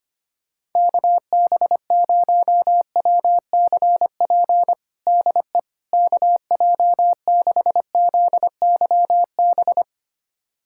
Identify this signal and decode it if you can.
Morse code, heard as a keyed tone, about 25 wpm, 700 Hz. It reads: KB0WCP DE KJ6ZYB
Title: USA Callsign DE Callsign - 6 Characters - 25wpm